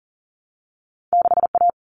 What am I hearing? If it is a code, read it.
Morse code, heard as a keyed tone, about 38 wpm, 700 Hz. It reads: 6A
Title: Common Cut Numbers - 2-Digit 40wpm